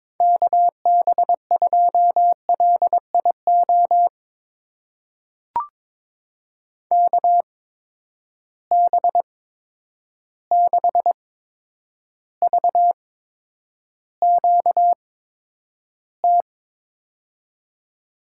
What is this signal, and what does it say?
Morse code, heard as a keyed tone, about 22 wpm, spaced as for 9 wpm, 700 Hz. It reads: KB2LIO KB6VQT
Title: USA Call Signs - 6 Characters - 8x Character Spacing - 22wpm